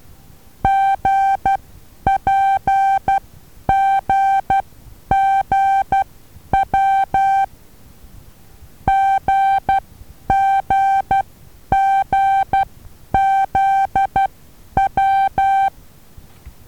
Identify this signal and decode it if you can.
Morse code, heard as a keyed tone, about 12 wpm, spaced as for 9 wpm, 780 Hz. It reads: GPGGW GGGZW
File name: Taviro HA6VA B